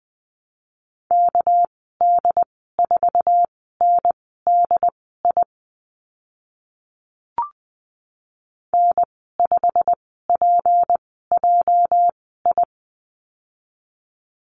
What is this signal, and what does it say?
Morse code, heard as a keyed tone, about 20 wpm, spaced as for 14 wpm, 700 Hz. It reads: KD4NDI N5PJI
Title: 219 Lesson 35g - Number 0 - Call Signs 20wpm